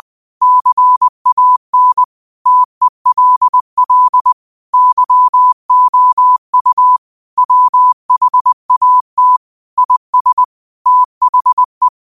Morse code, heard as a keyed tone, about 20 wpm, 1000 Hz. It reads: CAN TELL YOU WHAT IS THE